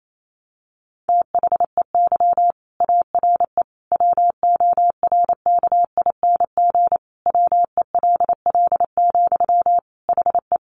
Morse code, heard as a keyed tone, about 28 wpm, 700 Hz. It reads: THEY ARE WORKING WELL, HE
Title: Sentences from Top 700 Words 28wpm